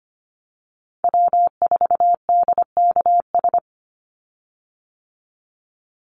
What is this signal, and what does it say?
Morse code, heard as a keyed tone, about 25 wpm, 700 Hz. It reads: W4DKS
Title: Call Signs - 25wpm